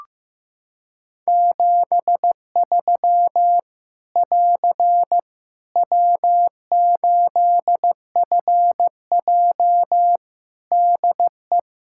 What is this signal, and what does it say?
Morse code, heard as a keyed tone, about 16 wpm, 700 Hz. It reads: 73 <AR> W8FJ DE
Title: Practice QSO - Exchange 4 - Ending - 15wpm